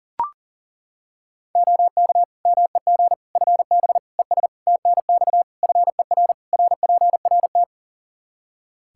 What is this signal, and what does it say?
Morse code, heard as a keyed tone, about 40 wpm, 700 Hz. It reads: OK MEG FB ES TNX FER RPRT
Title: Practice QSO - Exchange 2 - Info - 40wpm